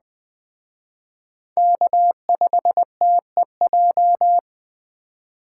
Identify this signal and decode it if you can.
Morse code, heard as a keyed tone, about 20 wpm, 700 Hz. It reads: K5TEJ
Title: Speed Racing - USA 1x3 Call Signs 17wpm